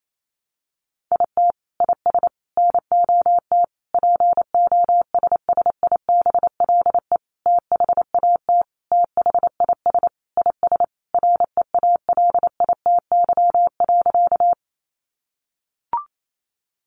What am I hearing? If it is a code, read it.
Morse code, heard as a keyed tone, about 28 wpm, 700 Hz. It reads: IT IS NOT POSSIBLE THAT THIS IS REALITY.
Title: Sentences from Top 1200 Words 28wpm